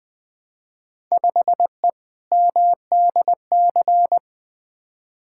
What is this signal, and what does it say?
Morse code, heard as a keyed tone, about 20 wpm, 700 Hz. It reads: 5E MDC
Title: Field Day Exchanges - 20wpm